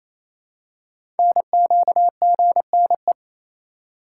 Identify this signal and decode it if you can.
Morse code, heard as a keyed tone, about 28 wpm, 700 Hz. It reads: NQGNE
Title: Code Groups - 5 Letters - 28wpm